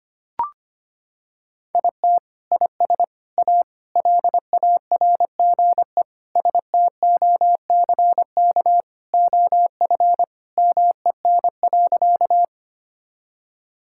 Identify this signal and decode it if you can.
Morse code, heard as a keyed tone, about 25 wpm, 700 Hz. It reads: IT IS A LARGE STOCK OF MEN.
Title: Sentences from Top 1000 Words - Review Part 5 - 25wpm